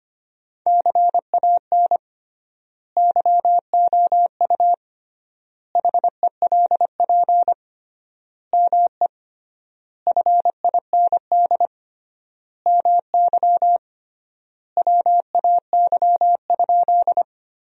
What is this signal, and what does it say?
Morse code, heard as a keyed tone, about 25 wpm, 700 Hz. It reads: CAN YOU HELP ME FIND MY WAY?
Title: 3x Word-Spacing - Sentences from Top 200 Words - 25wpm